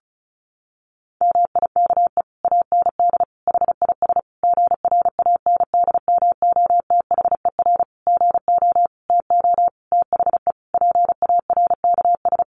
Morse code, heard as a keyed tone, about 34 wpm, 700 Hz. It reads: MIKE AND HIS GRANDMOTHER GO TO THE PARKS
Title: Additional Easy Phrases 35wpm